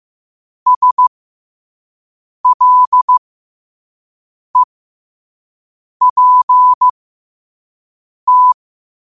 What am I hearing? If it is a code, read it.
Morse code, heard as a keyed tone, about 15 wpm, spaced as for 5 wpm, 1000 Hz. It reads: SLEPT
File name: 5wpm